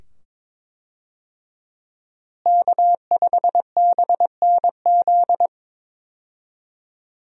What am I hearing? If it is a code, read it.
Morse code, heard as a keyed tone, about 22 wpm, 700 Hz. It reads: K5BNZ
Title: USA Call Signs - 5 Characters - Repeated 2x - 22wpm